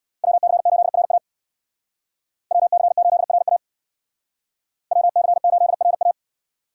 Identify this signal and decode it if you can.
Morse code, heard as a keyed tone, about 64 wpm, 700 Hz. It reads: WC8RA WC8RA WC8RA